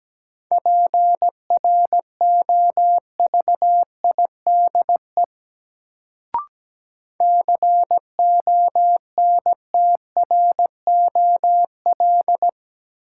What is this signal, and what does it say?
Morse code, heard as a keyed tone, about 17 wpm, 700 Hz. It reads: PROVIDE CONTROL
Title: Sentences from Top 1000 Words - Review Part 2 - New Word Drill - 17wpm